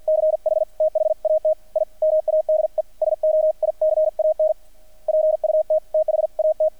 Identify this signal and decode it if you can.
Morse code, heard as a keyed tone, about 50 wpm, 620 Hz, with radio noise. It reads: CH THAT I MADE SO I CAN PUT THAT
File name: archived / CRSnet